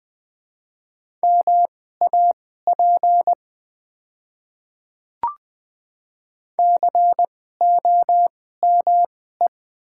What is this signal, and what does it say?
Morse code, heard as a keyed tone, about 20 wpm, spaced as for 14 wpm, 700 Hz. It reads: MAP COME
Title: 2x Character Spacing - Top 500 Words - 20wpm